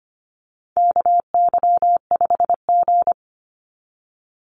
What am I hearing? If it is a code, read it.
Morse code, heard as a keyed tone, about 25 wpm, 700 Hz. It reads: KY5G
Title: USA Call Signs 25wpm